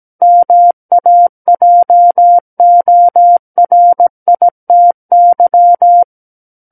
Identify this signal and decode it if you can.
Morse code, heard as a keyed tone, about 17 wpm, 700 Hz. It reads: MAJORITY
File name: Book One / 08-Chapter 8 - 17 WPM